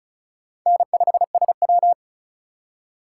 Morse code, heard as a keyed tone, about 45 wpm, 700 Hz. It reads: N5SW